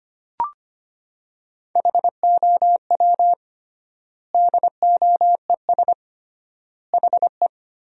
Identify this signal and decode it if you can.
Morse code, heard as a keyed tone, about 25 wpm, 700 Hz. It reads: HOW DOES HE
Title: 3x Word-Spacing - Sentences from Top 200 Words - 25wpm